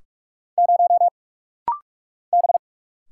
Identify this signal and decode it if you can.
Morse code, heard as a keyed tone, about 44 wpm, 700 Hz. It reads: 0 B
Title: Single Letter-Number Alt - Rapid-Fire 45wpm